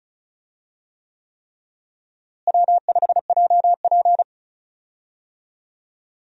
Morse code, heard as a keyed tone, about 34 wpm, 700 Hz. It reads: W5JP
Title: USA Call Signs - 4 Characters - Repeated 3x - 35wpm